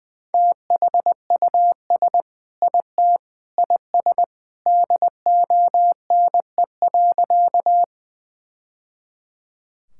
Morse code, heard as a keyed tone, about 19 wpm, 700 Hz. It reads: THUS IT IS DONE.
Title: Sentences from Top 1000 Words - Review Part 5 - 20wpm